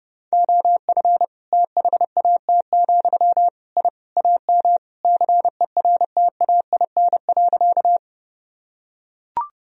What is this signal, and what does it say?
Morse code, heard as a keyed tone, about 30 wpm, 700 Hz. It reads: OF THAT, I AM CERTAIN.
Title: Sentences from Top 1000 Words - Review Part 2 - 30wpm